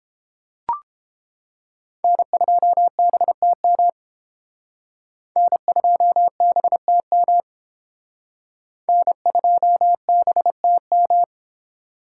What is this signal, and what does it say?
Morse code, heard as a keyed tone, about 32 wpm, 700 Hz. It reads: N2BTM N2BTM N2BTM